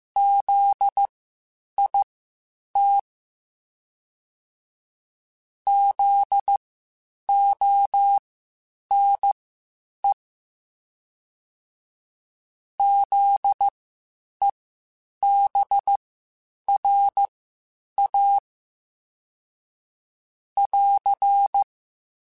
Morse code, heard as a keyed tone, about 15 wpm, spaced as for 8 wpm, 800 Hz. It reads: ZIT ZONE ZEBRA <AR>
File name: Spaced Alphas / Spaced Alphas mp3s / Zs sp @13